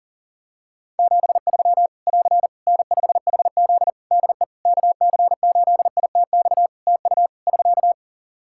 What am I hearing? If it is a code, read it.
Morse code, heard as a keyed tone, about 40 wpm, 700 Hz. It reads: 73 <AR> N5HZ DE KC8ITX TU <SK>
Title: Practice QSO - Exchange 4 - Ending - 40wpm